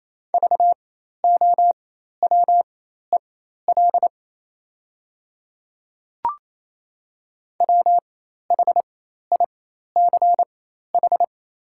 Morse code, heard as a keyed tone, about 28 wpm, spaced as for 13 wpm, 700 Hz. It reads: VOWEL WHICH
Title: 4x Character Spacing - Top 500 Words - 28wpm